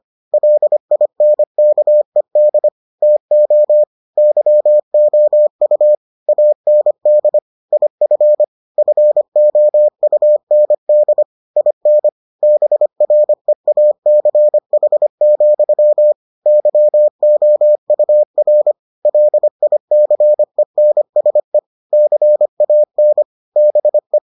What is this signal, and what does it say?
Morse code, heard as a keyed tone, about 25 wpm, 600 Hz. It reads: LINKED TO YOU AND IF FOUND IN BREACH, YOUR LICENSE CAN BE